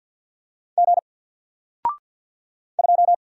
Morse code, heard as a keyed tone, about 50 wpm, 700 Hz. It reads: G 2
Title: Single Letter-Number 50wpm